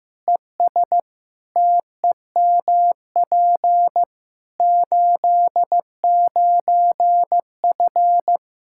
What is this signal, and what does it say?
Morse code, heard as a keyed tone, about 16 wpm, 700 Hz. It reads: ES TEMP 89F
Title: Practice QSO - Exchange 2 - Info - 15wpm